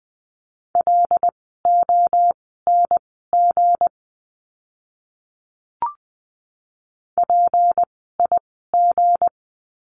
Morse code, heard as a keyed tone, about 20 wpm, spaced as for 15 wpm, 700 Hz. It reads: LONG PIG
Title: 160 Lesson 26f - Letter g - Words 20wpm